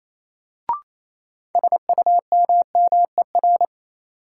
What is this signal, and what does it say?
Morse code, heard as a keyed tone, about 28 wpm, 700 Hz. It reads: SUMMER
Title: Sentences from Top 1000 Words - Review Part 6 - New Word Drill - 28wpm